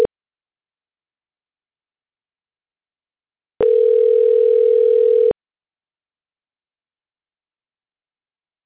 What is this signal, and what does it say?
Ringback tone (North America), 440 + 480 Hz.